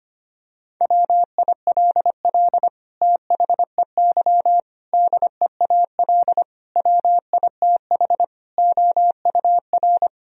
Morse code, heard as a keyed tone, about 25 wpm, 700 Hz. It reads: WILL THEY DEAL WITH OUR
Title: Sentences from Top 1000 Words - Review Part 4 - 25wpm